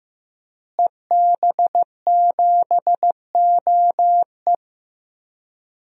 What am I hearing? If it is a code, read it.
Morse code, heard as a keyed tone, about 15 wpm, 700 Hz. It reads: EB7OE
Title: DXCC-100 Call Signs - 5 Characters - Repeated 2x - 15wpm